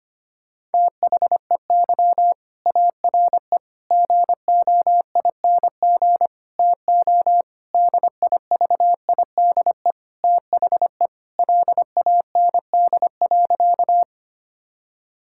Morse code, heard as a keyed tone, about 25 wpm, 700 Hz. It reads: THEY ARE GOING TO DIVIDE THE LAND.
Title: Sentences from Top 1600 Words 25wpm